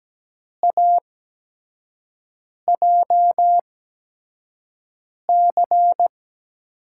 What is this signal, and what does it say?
Morse code, heard as a keyed tone, about 17 wpm, spaced as for 5 wpm, 700 Hz. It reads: AJC
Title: USA Call Signs - 5 Characters - 8x Character Spacing - 17wpm